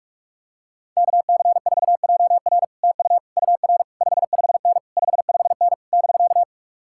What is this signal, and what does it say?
Morse code, heard as a keyed tone, about 48 wpm, 700 Hz. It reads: KC4JR TU UR 55N 55N <BK>